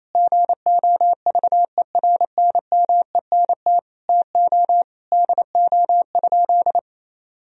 Morse code, heard as a keyed tone, about 28 wpm, 700 Hz. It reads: GOVERNMENT TO DO?
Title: Sentences from Top 1000 Words - Review Part 3 - 28wpm